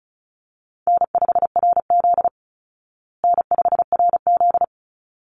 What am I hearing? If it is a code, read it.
Morse code, heard as a keyed tone, about 34 wpm, 700 Hz. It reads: N5RZ N5RZ
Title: Easy Call Signs Repeated 2x - 35wpm